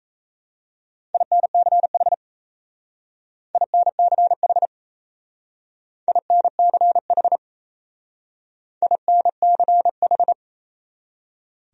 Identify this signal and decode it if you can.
Morse code, heard as a keyed tone, about 34 wpm, 700 Hz. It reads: INCH INCH INCH INCH
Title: Speed-Racing Top 500 Words 28wpm